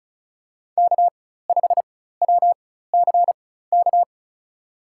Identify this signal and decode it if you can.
Morse code, heard as a keyed tone, about 36 wpm, spaced as for 16 wpm, 700 Hz. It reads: K5WCK
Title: USA Call Signs - 5 Characters - 4x Character Spacing - 35wpm